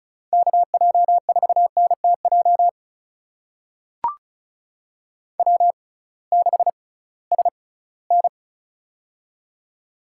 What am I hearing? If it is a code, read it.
Morse code, heard as a keyed tone, about 36 wpm, spaced as for 20 wpm, 700 Hz. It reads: KJ4NTJ W6SN
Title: USA Call Signs - 6x Character Spacing - 35wpm